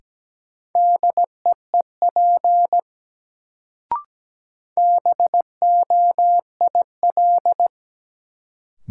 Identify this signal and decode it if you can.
Morse code, heard as a keyed tone, about 17 wpm, 700 Hz. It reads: DEEP BOIL
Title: Sets of 5 Words Encore - New Word Drill - 17wpm